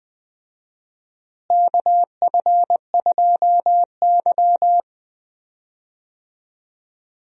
Morse code, heard as a keyed tone, about 20 wpm, 700 Hz. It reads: KF2Y
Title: USA Call Signs - 4 Characters - Repeated 2x - 20wpm